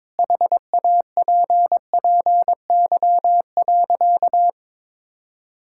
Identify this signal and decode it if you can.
Morse code, heard as a keyed tone, about 22 wpm, 700 Hz. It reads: HAPPY.
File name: practice / 22wpm / Sentences with 2-Word Phrases - 22wpm